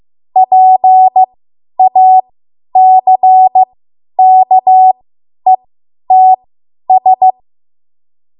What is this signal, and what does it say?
Morse code, heard as a keyed tone, about 15 wpm, spaced as for 10 wpm, 750 Hz. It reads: PACKETS
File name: Morse / 10WPM